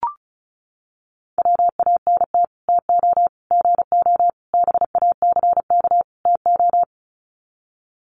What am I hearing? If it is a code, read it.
Morse code, heard as a keyed tone, about 34 wpm, 700 Hz. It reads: WANT TO GO BACK TO
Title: Sets of 5 Words 35wpm